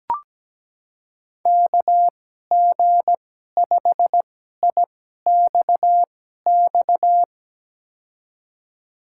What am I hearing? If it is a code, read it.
Morse code, heard as a keyed tone, about 17 wpm, spaced as for 12 wpm, 700 Hz. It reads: KG5IXX
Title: USA Call Signs - 2x Character Spacing - 17wpm